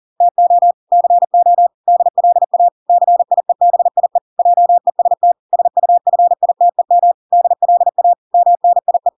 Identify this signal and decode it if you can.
Morse code, heard as a keyed tone, about 40 wpm, 700 Hz. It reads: TO CO DLA CIEBIE JEST SUFITEM DLA MNIE